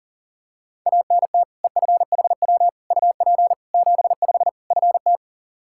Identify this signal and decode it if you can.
Morse code, heard as a keyed tone, about 37 wpm, 700 Hz. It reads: ANT EFHW UP 75 FT
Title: K1USN SST QSO Practice 40wpm